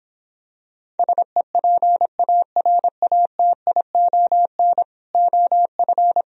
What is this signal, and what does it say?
Morse code, heard as a keyed tone, about 26 wpm, 700 Hz. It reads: SEPARATION OF